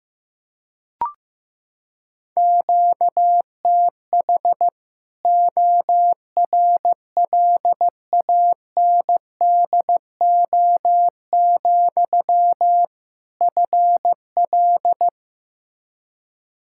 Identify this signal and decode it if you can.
Morse code, heard as a keyed tone, about 15 wpm, 700 Hz. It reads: QTH ORLANDO, FL
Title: Practice QSO - Exchange 1 - Beginning - 15wpm